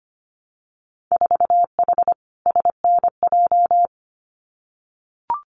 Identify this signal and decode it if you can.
Morse code, heard as a keyed tone, about 25 wpm, 690 Hz. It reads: 4H SNJ
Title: Winter Field Day Exchanges Encore 25wpm